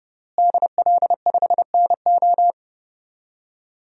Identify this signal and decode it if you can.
Morse code, heard as a keyed tone, about 31 wpm, 700 Hz. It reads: DL5NO
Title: DXCC-100 Call Signs - 5 Characters - Repeated 3x - 30wpm